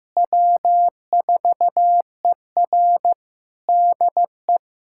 Morse code, heard as a keyed tone, about 15 wpm, 700 Hz. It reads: W4ER DE
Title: K1USN SST QSO Practice 15wpm